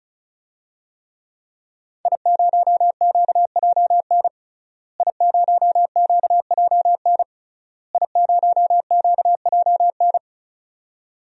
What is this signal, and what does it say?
Morse code, heard as a keyed tone, about 34 wpm, 700 Hz. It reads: I0QJN I0QJN I0QJN
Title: DXCC-100 Call Signs - 5 Characters - Repeated 3x - 35wpm